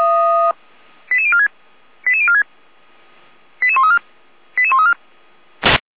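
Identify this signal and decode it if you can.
Five-tone selective call (ZVEI), heard as quick runs of five tones, tones of about 70 ms. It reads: 88046 88046 88134 88134
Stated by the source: ZVEI five-tone call received by SDR.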